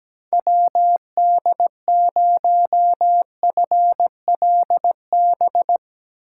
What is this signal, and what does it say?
Morse code, heard as a keyed tone, about 18 wpm, 700 Hz. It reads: WD0FLB
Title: Speed Racing - USA 2x3 Call Signs 17wpm